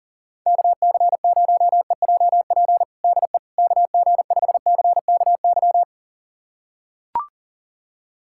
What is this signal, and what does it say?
Morse code, heard as a keyed tone, about 40 wpm, 700 Hz. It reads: KC0EJP DE KG5CKY
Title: USA Callsign DE Callsign - 6 Characters - 40wpm